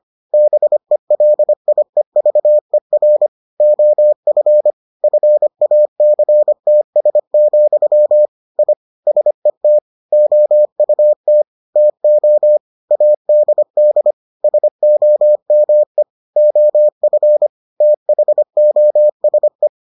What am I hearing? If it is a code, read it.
Morse code, heard as a keyed tone, about 25 wpm, 600 Hz. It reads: BELIEVER OF FACTS, I SET OUT TO ADD SOME OF THOSE